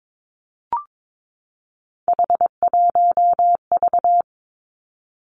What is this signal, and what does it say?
Morse code, heard as a keyed tone, about 22 wpm, 700 Hz. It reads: H1V
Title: Three Challenging Letter-Number Combinations 22wpm